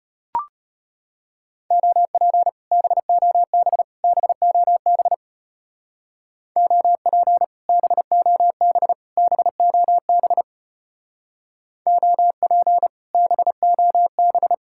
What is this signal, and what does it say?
Morse code, heard as a keyed tone, about 30 wpm, 700 Hz. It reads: OP BOB BOB OP BOB BOB OP BOB
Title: Speed-Racing Common QSO Phrases - 25wpm